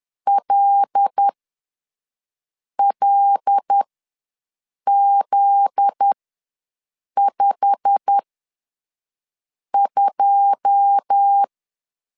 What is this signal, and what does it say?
Morse code, heard as a keyed tone, about 10 wpm, spaced as for 5 wpm, 800 Hz. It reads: LLZ52
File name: LEZIONI / 9 2867 ws 10wpm 4215